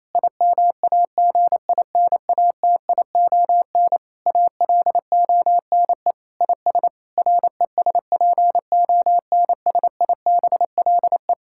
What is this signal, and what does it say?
Morse code, heard as a keyed tone, about 28 wpm, 700 Hz. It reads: IMAGINATION ALONE IS RESPONSIBLE
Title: Sentences from Top 1600 Words 28wpm